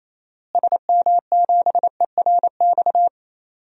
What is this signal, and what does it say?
Morse code, heard as a keyed tone, about 28 wpm, 700 Hz. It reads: SM7ERX
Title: DXCC-100 Call Signs - 6 Characters - Repeated 3x - 28wpm